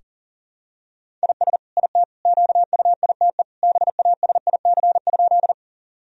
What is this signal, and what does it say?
Morse code, heard as a keyed tone, about 37 wpm, 700 Hz. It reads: IS IT QUITE BASIC?
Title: Sentences from Top 1000 Words - Review Part 3 - 40wpm